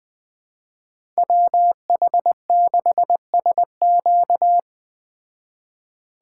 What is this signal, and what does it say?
Morse code, heard as a keyed tone, about 19 wpm, 700 Hz. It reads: WH6SQ